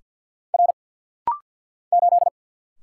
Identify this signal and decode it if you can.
Morse code, heard as a keyed tone, about 50 wpm, 700 Hz. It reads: R 8
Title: Single Letter-Number - Rapid-Fire 50wpm